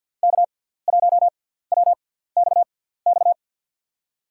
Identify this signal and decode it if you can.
Morse code, heard as a keyed tone, about 51 wpm, spaced as for 17 wpm, 700 Hz. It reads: K1WXX
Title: USA Call Signs - 5 Characters - 6x Character Spacing - 50wpm